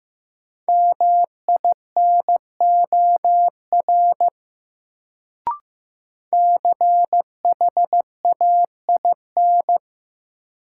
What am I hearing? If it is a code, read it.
Morse code, heard as a keyed tone, about 15 wpm, 700 Hz. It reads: MINOR CHAIN
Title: Sentences from Top 1500 Words - New Word Drill - 15wpm